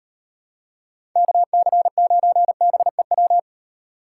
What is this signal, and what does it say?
Morse code, heard as a keyed tone, about 38 wpm, 700 Hz. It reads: KC9BEW